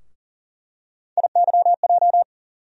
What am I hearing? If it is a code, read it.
Morse code, heard as a keyed tone, about 40 wpm, 700 Hz. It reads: IYJ
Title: USA Call Sign Postfixes - 3 Letters 40wpm